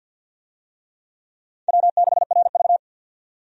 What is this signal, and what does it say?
Morse code, heard as a keyed tone, about 53 wpm, 700 Hz. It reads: W6RV
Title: USA Call Signs - 4 Characters - 50wpm